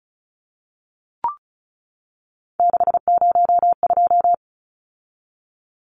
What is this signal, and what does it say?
Morse code, heard as a keyed tone, about 34 wpm, 700 Hz. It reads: 602
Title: Numbers - 3-Digit 35wpm